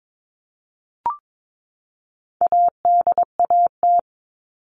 Morse code, heard as a keyed tone, about 22 wpm, 700 Hz. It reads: ADAT
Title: All Cut Numbers - 4-Digit 22wpm